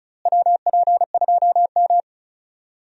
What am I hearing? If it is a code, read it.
Morse code, heard as a keyed tone, about 36 wpm, 700 Hz. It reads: WP2M
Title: USA Call Signs - 4 Characters - Repeated 3x - 35wpm